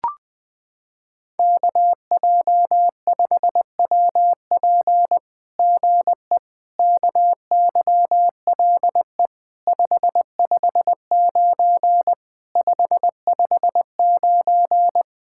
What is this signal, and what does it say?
Morse code, heard as a keyed tone, about 20 wpm, 700 Hz. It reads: KJ5WP GE KYLE 559 559
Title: SOTA QSO Practice 20wpm